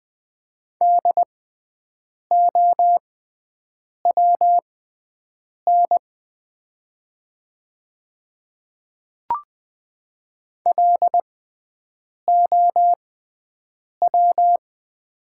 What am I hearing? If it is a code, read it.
Morse code, heard as a keyed tone, about 20 wpm, spaced as for 7 wpm, 700 Hz. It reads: DOWN LOW
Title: 6x Character Spacing - Top 300 Words - 20wpm